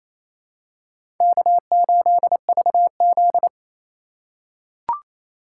Morse code, heard as a keyed tone, about 28 wpm, 700 Hz. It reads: K8VZ
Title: USA Call Signs - 4x Character Spacing - 28wpm